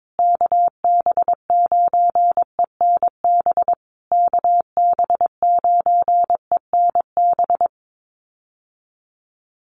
Morse code, heard as a keyed tone, about 22 wpm, 700 Hz. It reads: KB9ENB KB9ENB
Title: USA Call Signs Repeated 3x 22wpm